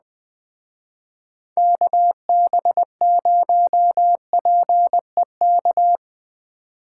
Morse code, heard as a keyed tone, about 20 wpm, 700 Hz. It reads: KB0PEK
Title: Speed Racing - USA Call Signs 17wpm